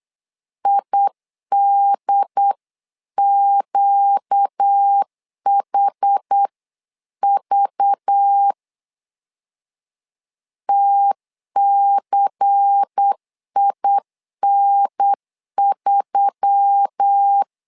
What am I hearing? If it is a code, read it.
Morse code, heard as a keyed tone, about 8 wpm, 800 Hz. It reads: IDQHV TCIN3
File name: LEZIONI / 5 LBVC / 5 LBVC 8wpm 2935